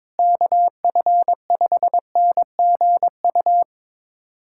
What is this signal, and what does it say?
Morse code, heard as a keyed tone, about 22 wpm, 700 Hz. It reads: KF5NGU
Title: USA Callsign DE Callsign - 6 Characters - 22wpm